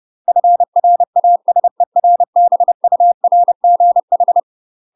Morse code, heard as a keyed tone, about 30 wpm, 700 Hz. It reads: FRASERBURGH
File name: UK MASTER QSOs TRAINING / 30WPM / UK MASTER QSOs 0049